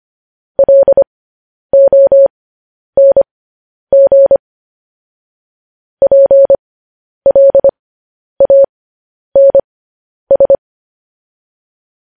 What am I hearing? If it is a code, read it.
Morse code, heard as a keyed tone, about 25 wpm, spaced as for 10 wpm, 550 Hz. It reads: LONG PLANS
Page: news headlines in morse code at 10 wpm